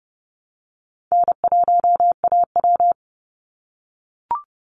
Morse code, heard as a keyed tone, about 30 wpm, 700 Hz. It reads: N1AW
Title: NA SOTA Chaser Callsigns 30wpm